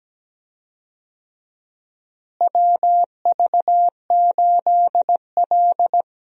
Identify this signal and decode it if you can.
Morse code, heard as a keyed tone, about 17 wpm, 700 Hz. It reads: WV8L
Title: USA Call Signs - 4 Characters - Repeated 2x - 17wpm